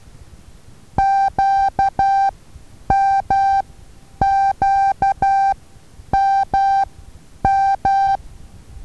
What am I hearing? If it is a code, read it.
Morse code, heard as a keyed tone, about 12 wpm, spaced as for 9 wpm, 780 Hz. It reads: QMQMM
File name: Taviro HA6VA A